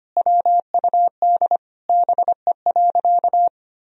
Morse code, heard as a keyed tone, about 25 wpm, 700 Hz. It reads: WUD BE.
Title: Sentences from Top 200 Words and Abbreviations 25wpm